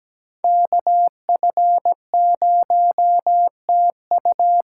Morse code, heard as a keyed tone, about 20 wpm, 700 Hz. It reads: KF0TU
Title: Speed Racing - USA 2x2 Call Signs 17wpm